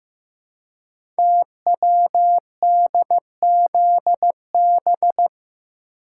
Morse code, heard as a keyed tone, about 15 wpm, 700 Hz. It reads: TWDZB
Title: Code Groups - 5 Letters - 15wpm